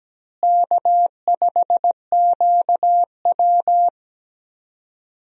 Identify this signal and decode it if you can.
Morse code, heard as a keyed tone, about 17 wpm, 700 Hz. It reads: K5QW
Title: USA Call Signs - 4 Characters - Repeated 3x - 17wpm